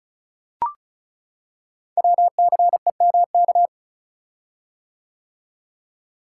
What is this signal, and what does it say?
Morse code, heard as a keyed tone, about 36 wpm, 700 Hz. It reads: WCEMK
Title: Code Groups - 5 Letters - 35wpm